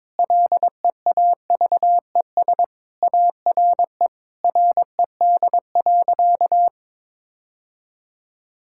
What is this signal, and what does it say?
Morse code, heard as a keyed tone, about 22 wpm, 700 Hz. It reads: LEAVES ARE RED.
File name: practice / 22wpm / Easy Sentences with Hints - 22wpm